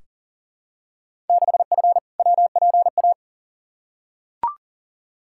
Morse code, heard as a keyed tone, about 40 wpm, 700 Hz. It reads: 6F WPA
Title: Field Day Exchanges Encore - 40wpm